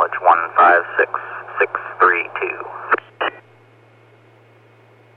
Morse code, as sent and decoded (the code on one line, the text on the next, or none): none